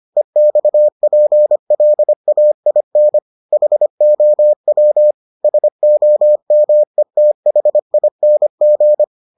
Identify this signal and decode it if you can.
Morse code, heard as a keyed tone, about 25 wpm, 600 Hz. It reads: EXPLAIN HOW SOMETHING